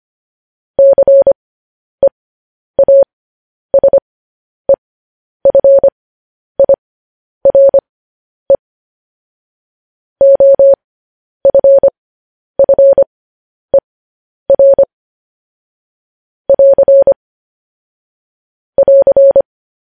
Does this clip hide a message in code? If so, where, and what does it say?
Yes: Morse code from 0.8 to 19.4 s, heard as a keyed tone, about 25 wpm, spaced as for 10 wpm, 550 Hz. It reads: CEASEFIRE OFFER <AR> <AR>